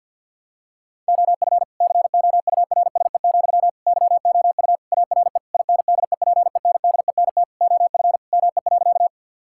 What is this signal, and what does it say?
Morse code, heard as a keyed tone, about 50 wpm, 700 Hz. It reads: OF COURSE, YOU ARE INDEPENDENT OF ME.
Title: Sentences from Top 1000 Words - Review Part 6 - 50wpm